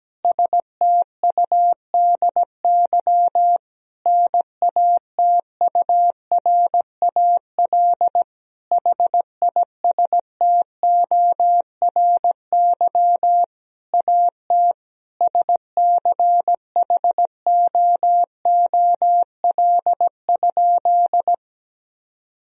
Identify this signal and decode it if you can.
Morse code, heard as a keyed tone, about 17 wpm, 700 Hz. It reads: STUDY NATURAL HISTORY AT SCHOOL?
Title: Sentences from Top 1000 Words - Review Part 4 - 17wpm